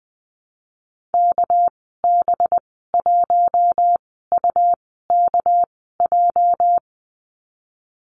Morse code, heard as a keyed tone, about 20 wpm, spaced as for 15 wpm, 700 Hz. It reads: KB1UKJ
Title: 213 Lesson 34g - Number 8 - Call Signs 20wpm